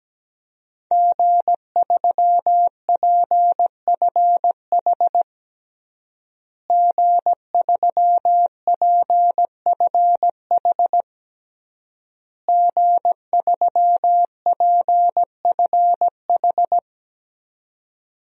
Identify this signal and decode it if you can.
Morse code, heard as a keyed tone, about 17 wpm, 700 Hz. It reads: G3PFH G3PFH G3PFH
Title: DXCC-100 Call Signs - 5 Characters - Repeated 3x - 17wpm